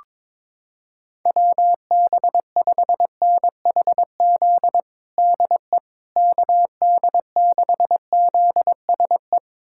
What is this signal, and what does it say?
Morse code, heard as a keyed tone, about 22 wpm, 700 Hz. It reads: WB5NHZ DE KD6ZSE
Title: USA Callsign DE Callsign 22wpm